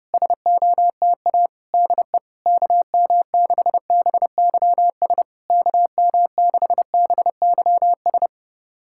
Morse code, heard as a keyed tone, about 30 wpm, 700 Hz. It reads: SOTA DE KM6BYS KM6BYS
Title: SOTA QSO Practice 30wpm